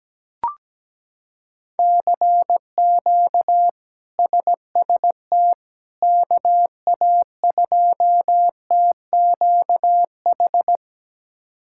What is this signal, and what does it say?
Morse code, heard as a keyed tone, about 17 wpm, 700 Hz. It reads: CQ SST KA2TQH
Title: K1USN SST QSO Practice - 17wpm